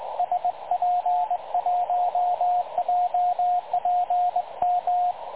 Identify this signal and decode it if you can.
Morse code, heard as a keyed tone, about 20 wpm, 720 Hz, with radio noise. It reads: SP1JPM
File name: dlazaawansowanych / nagraniazpasma / websdr recording 3565.0kHz